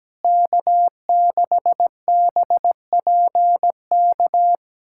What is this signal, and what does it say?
Morse code, heard as a keyed tone, about 17 wpm, 700 Hz. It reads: K6BPK
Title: USA Callsign DE Callsign - 5 Characters - 17wpm